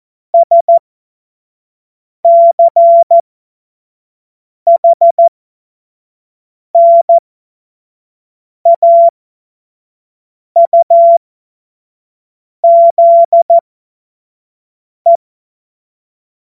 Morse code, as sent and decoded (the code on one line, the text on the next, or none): ... -.-. .... -. .- ..- --.. .
SCHNAUZE